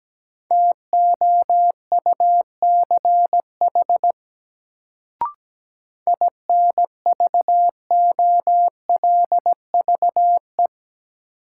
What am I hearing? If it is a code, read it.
Morse code, heard as a keyed tone, about 17 wpm, 700 Hz. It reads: TOUCH INVOLVE
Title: Sentences from Top 1200 Words - New Word Drill Alt - 17wpm